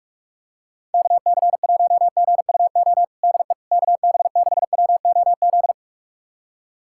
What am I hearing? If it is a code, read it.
Morse code, heard as a keyed tone, about 46 wpm, 700 Hz. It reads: KC1GUO DE KB6WOZ
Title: USA Callsign DE Callsign 45wpm